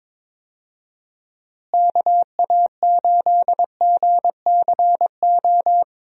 Morse code, heard as a keyed tone, about 22 wpm, 700 Hz. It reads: KA8GCO
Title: USA Call Signs - 6 Characters - Repeated 2x - 22wpm